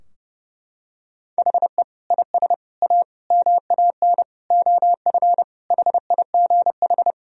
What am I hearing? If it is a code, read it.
Morse code, heard as a keyed tone, about 32 wpm, 700 Hz. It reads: HE IS A MAN OF HIGH
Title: Sentences from Top 1000 Words - Review Part 2 - 30wpm